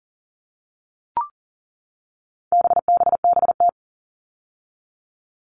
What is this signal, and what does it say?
Morse code, heard as a keyed tone, about 40 wpm, 700 Hz. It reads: BBBT
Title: All Cut Numbers - 4-Digit 40wpm